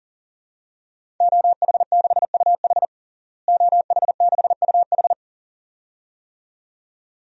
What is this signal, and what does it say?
Morse code, heard as a keyed tone, about 40 wpm, 700 Hz. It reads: OH6UH OH6UH
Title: DXCC-100 Call Signs - 5 Characters - Repeated 2x - 40wpm